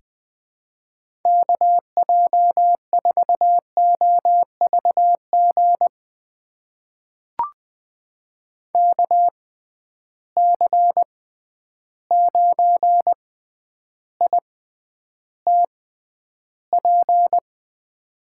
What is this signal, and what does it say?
Morse code, heard as a keyed tone, about 21 wpm, spaced as for 10 wpm, 700 Hz. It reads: KJ4OVG KC9ITP
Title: USA Call Signs - 6 Characters - 6x Character Spacing - 20wpm